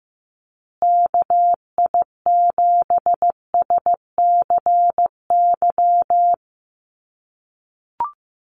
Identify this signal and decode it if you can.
Morse code, heard as a keyed tone, about 15 wpm, 700 Hz. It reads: KI7SCY
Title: USA Call Signs Repeated 3x 15wpm